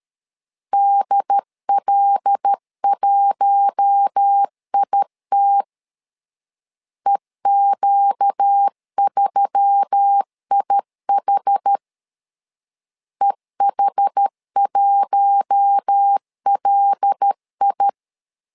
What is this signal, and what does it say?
Morse code, heard as a keyed tone, about 13 wpm, 800 Hz. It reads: DL1IT EQ3IH EH1LI
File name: LEZIONI / 5 LBVC / 5 L 12wpm 1913